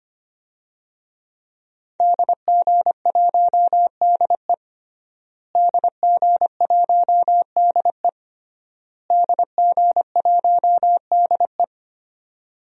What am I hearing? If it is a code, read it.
Morse code, heard as a keyed tone, about 25 wpm, 700 Hz. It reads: DG1DE DG1DE DG1DE
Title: DXCC-100 Call Signs - 5 Characters - Repeated 3x - 25wpm